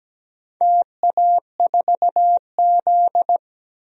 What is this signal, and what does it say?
Morse code, heard as a keyed tone, about 17 wpm, 700 Hz. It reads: TA4Z
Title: DXCC-100 Call Signs - 4 Characters - Repeated 2x - 17wpm